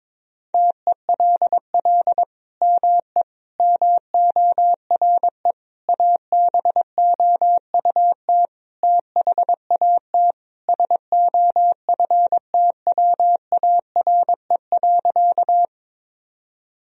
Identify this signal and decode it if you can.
Morse code, heard as a keyed tone, about 22 wpm, 700 Hz. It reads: TELL ME MORE ABOUT THAT SOFTWARE.
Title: Sentences from Top 1000 Words - Review Part 3 - 22wpm